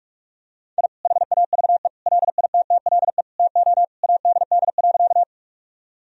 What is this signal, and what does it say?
Morse code, heard as a keyed tone, about 42 wpm, 700 Hz. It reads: I HAVE LITTLE TO ADD.
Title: Sentences from Top 1000 Words - Review Part 1 - 45wpm